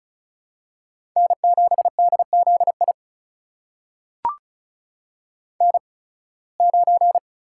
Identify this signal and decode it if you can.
Morse code, heard as a keyed tone, about 36 wpm, spaced as for 23 wpm, 700 Hz. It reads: N7DZI N9
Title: USA Call Signs - 5 Characters - 8x Character Spacing - 35wpm